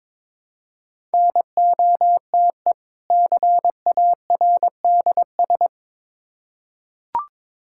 Morse code, heard as a keyed tone, about 22 wpm, 700 Hz. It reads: NOTE CARDS
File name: Difficult Words with Hints - 22wpm